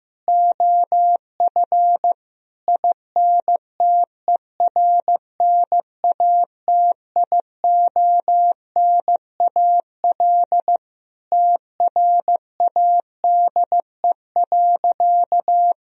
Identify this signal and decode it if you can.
Morse code, heard as a keyed tone, about 15 wpm, 700 Hz. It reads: OF INTERNATIONAL TRADE.
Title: Sentences from Top 1000 Words - Review Part 4 - 15wpm